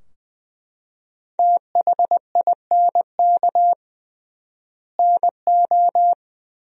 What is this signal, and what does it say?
Morse code, heard as a keyed tone, about 20 wpm, 700 Hz. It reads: THINK NO
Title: 3x Word-Spacing - Sentences from Top 200 Words - 20wpm